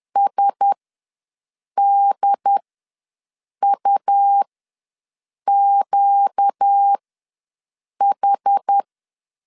Morse code, heard as a keyed tone, about 10 wpm, spaced as for 6 wpm, 800 Hz. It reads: SDUQH
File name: LEZIONI / 5 LBVC / 5 L ws 10wpm 3636